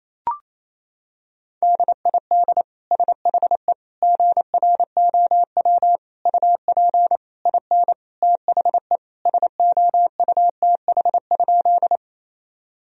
Morse code, heard as a keyed tone, about 27 wpm, 700 Hz. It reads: DID SHE GROW UP IN THE SOUTH?
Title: Sentences from Top 1000 Words - Review Part 2 - 28wpm